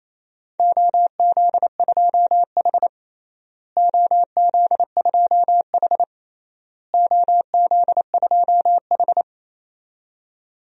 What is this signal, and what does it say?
Morse code, heard as a keyed tone, about 28 wpm, 700 Hz. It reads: OZ2H OZ2H OZ2H
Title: DXCC-100 Call Signs - 4 Characters - Repeated 3x - 28wpm